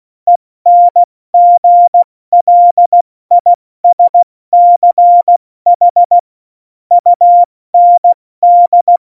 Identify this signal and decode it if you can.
Morse code, heard as a keyed tone, about 16 wpm, spaced as for 14 wpm, 700 Hz. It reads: ENGLISCH UND
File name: Morsekurs / Witze / Witz 20. November 2018, 70 F80